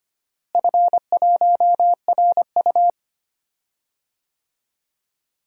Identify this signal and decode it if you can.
Morse code, heard as a keyed tone, about 25 wpm, 700 Hz. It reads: F1RU
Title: DXCC-100 Call Signs - 4 Characters - Repeated 3x - 25wpm